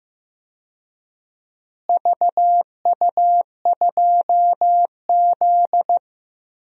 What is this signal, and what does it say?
Morse code, heard as a keyed tone, about 15 wpm, 700 Hz. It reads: VU2Z
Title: DXCC-100 Call Signs - 4 Characters - Repeated 3x - 15wpm